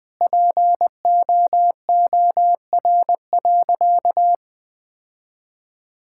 Morse code, heard as a keyed tone, about 20 wpm, 700 Hz. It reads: POOR.
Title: Sentences from Top 1000 Words - Review Part 6 - 20wpm